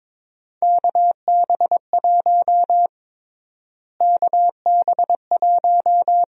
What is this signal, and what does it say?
Morse code, heard as a keyed tone, about 22 wpm, 700 Hz. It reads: KB1 KB1
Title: USA Call Sign Prefixes - Repeated 2x - 22wpm